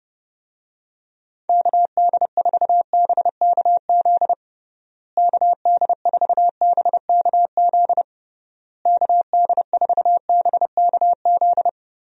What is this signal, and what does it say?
Morse code, heard as a keyed tone, about 32 wpm, 700 Hz. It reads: KD4BKZ KD4BKZ KD4BKZ
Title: USA Call Signs - 6 Characters - Repeated 3x - 30wpm